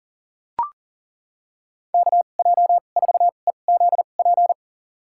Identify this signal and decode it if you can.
Morse code, heard as a keyed tone, about 40 wpm, spaced as for 29 wpm, 690 Hz. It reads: KJ4EZP
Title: USA Call Signs - 6 Characters - 2x Character Spacing - 40wpm